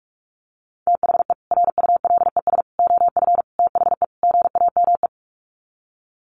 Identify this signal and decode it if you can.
Morse code, heard as a keyed tone, about 44 wpm, 700 Hz. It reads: THE RULES OF THE GAME
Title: Sets of 5 Words Encore2 45wpm